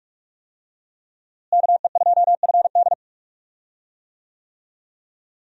Morse code, heard as a keyed tone, about 41 wpm, 700 Hz. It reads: KE2FD
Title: USA Call Signs - 5 Characters - Repeated 2x - 45wpm